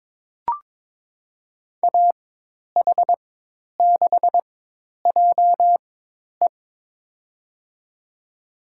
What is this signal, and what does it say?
Morse code, heard as a keyed tone, about 22 wpm, spaced as for 10 wpm, 700 Hz. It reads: AH6JE
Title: USA Call Signs - 5 Characters - 4x Character Spacing - 22wpm